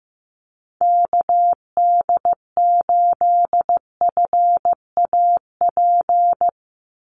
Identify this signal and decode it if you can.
Morse code, heard as a keyed tone, about 15 wpm, 700 Hz. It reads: KD8FAP
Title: USA Call Signs 15wpm